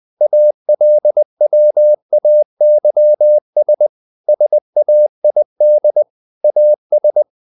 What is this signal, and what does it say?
Morse code, heard as a keyed tone, about 20 wpm, 600 Hz. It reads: ALWAYS SAID AS
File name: the old man and the sea / the old man and the sea part 1 20wpm